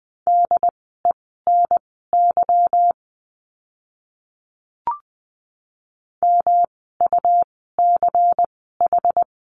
Morse code, heard as a keyed tone, about 20 wpm, spaced as for 15 wpm, 700 Hz. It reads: DENY MUCH
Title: 146 Lesson 24f - Letter y - Words 20wpm